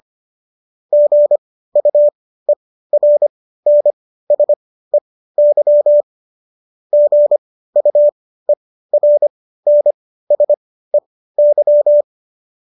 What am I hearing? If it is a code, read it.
Morse code, heard as a keyed tone, about 25 wpm, spaced as for 15 wpm, 590 Hz. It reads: GUERNSEY GUERNSEY